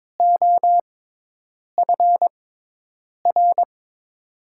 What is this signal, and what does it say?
Morse code, heard as a keyed tone, about 22 wpm, spaced as for 8 wpm, 700 Hz. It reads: OFR
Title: USA Call Signs - 6x Character Spacing - 22wpm